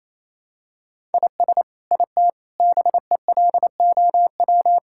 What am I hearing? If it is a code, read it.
Morse code, heard as a keyed tone, about 27 wpm, 700 Hz. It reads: IS IT BELOW
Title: Sentences from Top 1000 Words - Review Part 5 - 28wpm